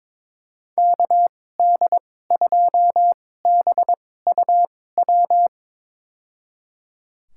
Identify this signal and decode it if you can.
Morse code, heard as a keyed tone, about 22 wpm, spaced as for 16 wpm, 700 Hz. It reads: KD2BUW
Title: USA Call Signs - 6 Characters - 2x Character Spacing - 22wpm